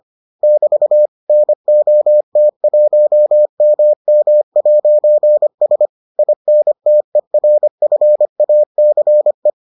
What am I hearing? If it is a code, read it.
Morse code, heard as a keyed tone, about 25 wpm, 600 Hz. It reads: <BT> NOT1MM'S INTERFACE